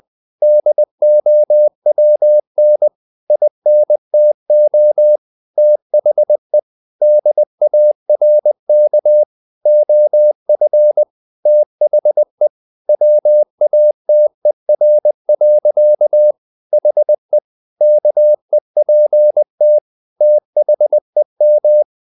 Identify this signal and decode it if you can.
Morse code, heard as a keyed tone, about 20 wpm, 600 Hz. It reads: DOWN INTO THE DARK OF THE WATER. HE KEPT THEM
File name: the old man and the sea / the old man and the sea part 1 20wpm